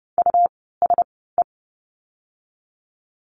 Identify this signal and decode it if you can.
Morse code, heard as a keyed tone, about 30 wpm, spaced as for 17 wpm, 700 Hz. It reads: USE
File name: course / 30:10wpm / 237 Lesson 38f - Letter q - Words 30wpm